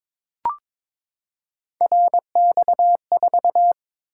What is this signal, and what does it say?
Morse code, heard as a keyed tone, about 22 wpm, 700 Hz. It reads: RX4
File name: practice / 22wpm / Three Letter-Number Combinations - 22wpm